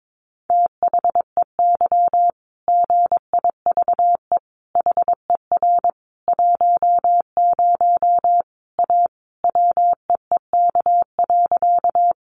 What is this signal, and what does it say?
Morse code, heard as a keyed tone, about 22 wpm, 700 Hz. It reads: THEY GIVE HER 10 A WEEK.
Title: Easy Phrases 22wpm